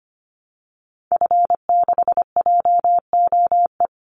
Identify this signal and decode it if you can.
Morse code, heard as a keyed tone, about 25 wpm, 700 Hz. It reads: F6JOE
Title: Call Signs Repeated 3x - 25wpm